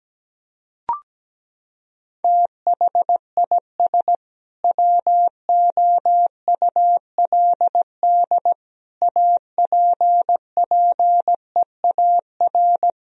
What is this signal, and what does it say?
Morse code, heard as a keyed tone, about 17 wpm, 700 Hz. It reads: THIS WOULD APPEAR
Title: Sentences from Top 1000 Words - Review Part 3 - 17wpm